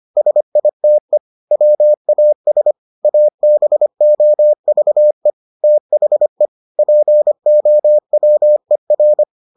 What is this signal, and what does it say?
Morse code, heard as a keyed tone, about 25 wpm, 600 Hz. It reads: SITE WAS ABOVE THE POWER